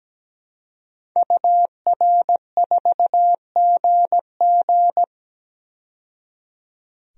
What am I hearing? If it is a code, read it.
Morse code, heard as a keyed tone, about 17 wpm, 700 Hz. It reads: UR4GG
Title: DXCC-100 Call Signs - 5 Characters - 17wpm